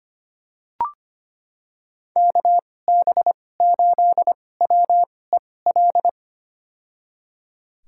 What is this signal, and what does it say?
Morse code, heard as a keyed tone, about 25 wpm, spaced as for 18 wpm, 700 Hz. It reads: KB8WEL
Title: USA Call Signs - 2x Character Spacing - 25wpm